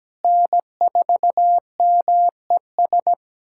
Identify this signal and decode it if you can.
Morse code, heard as a keyed tone, about 17 wpm, 700 Hz. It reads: N4MES
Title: USA Callsign DE Callsign - 5 Characters - 17wpm